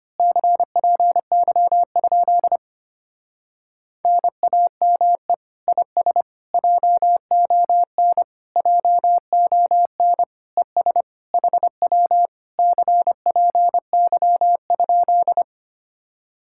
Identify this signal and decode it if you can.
Morse code, heard as a keyed tone, about 24 wpm, 700 Hz. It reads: CPY? NAME IS JON JON ES HW CPY?
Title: Speed-Racing Common QSO Phrases Encore - 25wpm